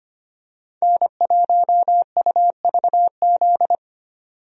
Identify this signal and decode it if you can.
Morse code, heard as a keyed tone, about 25 wpm, 700 Hz. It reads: N1UVZ
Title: USA Call Signs - 5 Characters - 25wpm